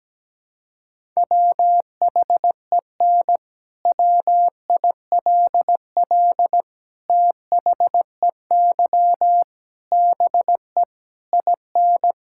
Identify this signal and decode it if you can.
Morse code, heard as a keyed tone, about 17 wpm, 700 Hz. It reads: WHEN WILL THEY BE IN
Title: Sentences from Top 1000 Words - Review Part 6 - 17wpm